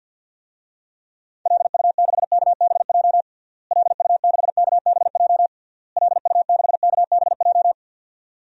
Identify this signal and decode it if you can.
Morse code, heard as a keyed tone, about 52 wpm, 700 Hz. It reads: LU6KBJ LU6KBJ LU6KBJ
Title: DXCC-100 Call Signs - 6 Characters - Repeated 3x - 50wpm